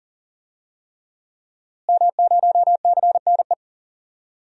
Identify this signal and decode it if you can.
Morse code, heard as a keyed tone, about 39 wpm, 700 Hz. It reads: M0CNE
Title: DXCC-100 Call Signs - 5 Characters - Repeated 3x - 40wpm